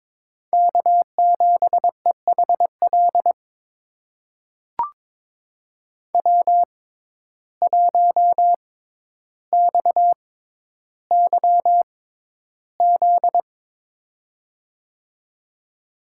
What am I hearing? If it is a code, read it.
Morse code, heard as a keyed tone, about 22 wpm, spaced as for 11 wpm, 700 Hz. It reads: K7EHL W1XYZ
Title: USA Call Signs - 5 Characters - 6x Character Spacing - 22wpm